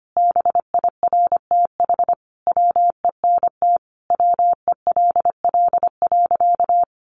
Morse code, heard as a keyed tone, about 25 wpm, 700 Hz. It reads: BIRTH WENT WELL.
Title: Sentences from Top 800 Words 25wpm